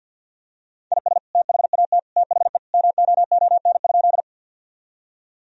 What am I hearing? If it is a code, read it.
Morse code, heard as a keyed tone, about 50 wpm, 700 Hz. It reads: IS THAT THE MOON?
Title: Sentences from Top 1000 Words - Review Part 3 - 50wpm